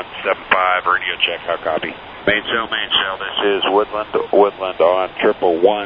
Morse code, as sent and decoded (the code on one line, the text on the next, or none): none